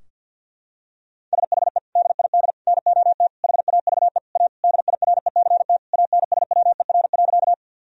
Morse code, heard as a keyed tone, about 51 wpm, 700 Hz. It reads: SHE DID NOT HAVE A DIRECT ANSWER.
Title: Sentences from Top 1000 Words - Review Part 2 - 50wpm